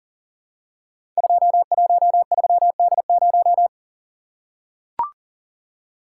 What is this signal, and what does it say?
Morse code, heard as a keyed tone, about 44 wpm, 690 Hz. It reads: 213D0
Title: Code Groups - 5 Letter:Numbers - 40wpm